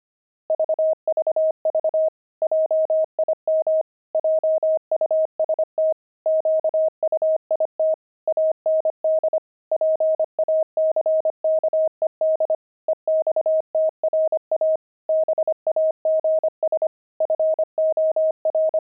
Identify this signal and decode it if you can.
Morse code, heard as a keyed tone, about 25 wpm, 630 Hz. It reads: VVV JIM JUST QUIT AND PACKED EXTRA BAGS FOR